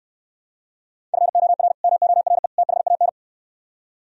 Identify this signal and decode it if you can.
Morse code, heard as a keyed tone, about 63 wpm, 700 Hz. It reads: FOR MORE THAN